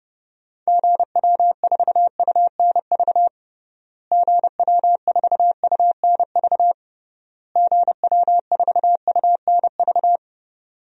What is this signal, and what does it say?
Morse code, heard as a keyed tone, about 30 wpm, 700 Hz. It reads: GW4UNV GW4UNV GW4UNV
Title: DXCC-100 Call Signs - 6 Characters - Repeated 3x - 30wpm